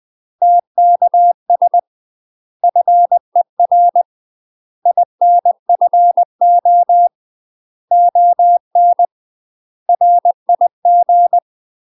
Morse code, heard as a keyed tone, about 20 wpm, 700 Hz. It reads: TKS FER INFO ON RIG
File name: UK MASTER QSOs TRAINING / 20WPM / UK MASTER QSOs 0055